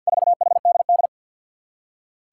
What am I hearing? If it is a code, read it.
Morse code, heard as a keyed tone, about 48 wpm, 700 Hz. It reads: 4HDD
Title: Four Challenging Letter-Number Combinations 50wpm